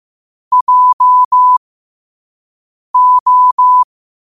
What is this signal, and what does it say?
Morse code, heard as a keyed tone, about 15 wpm, spaced as for 5 wpm, 1000 Hz. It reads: JO
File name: qotdfiles / 5wpm